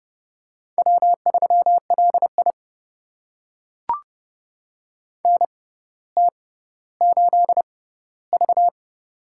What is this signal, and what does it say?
Morse code, heard as a keyed tone, about 30 wpm, spaced as for 15 wpm, 700 Hz. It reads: W3LI NT8V
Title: USA Call Signs - 4 Characters - 6x Character Spacing - 30wpm